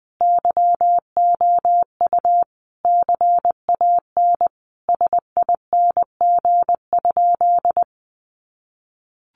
Morse code, heard as a keyed tone, about 20 wpm, 700 Hz. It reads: YOU CAN SING?
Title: Sentences from Top 400 Words 20wpm